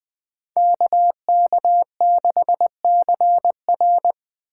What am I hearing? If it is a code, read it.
Morse code, heard as a keyed tone, about 21 wpm, 700 Hz. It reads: KK6CR